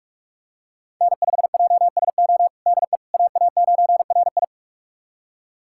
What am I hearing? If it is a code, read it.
Morse code, heard as a keyed tone, about 47 wpm, 700 Hz. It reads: N5JSO DE AA9RI
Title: USA Callsign DE Callsign - 5 Characters - 45wpm